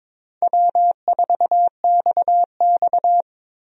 Morse code, heard as a keyed tone, about 22 wpm, 700 Hz. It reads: W4XX
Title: USA Call Signs - 4 Characters - 22wpm